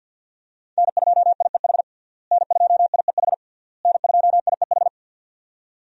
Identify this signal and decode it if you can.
Morse code, heard as a keyed tone, about 47 wpm, 700 Hz. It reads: N2IEH N2IEH N2IEH
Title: USA Call Signs - 5 Characters - Repeated 3x - 50wpm